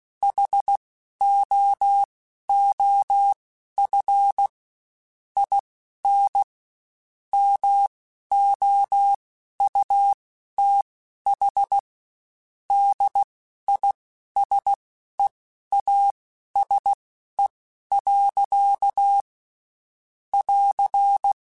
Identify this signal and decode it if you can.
Morse code, heard as a keyed tone, about 16 wpm, spaced as for 11 wpm, 800 Hz. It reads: HOOF IN MOUTH DISEASE. <AR>
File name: Quotes Spaced / Quotes Spaced-mp3s / HoofInMouth sp @15